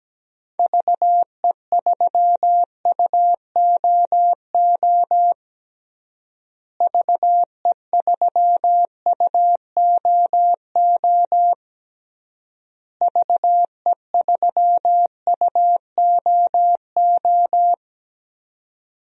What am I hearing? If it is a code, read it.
Morse code, heard as a keyed tone, about 17 wpm, 700 Hz. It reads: VE3UOO VE3UOO VE3UOO
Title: DXCC-100 Call Signs - 6 Characters - Repeated 3x - 17wpm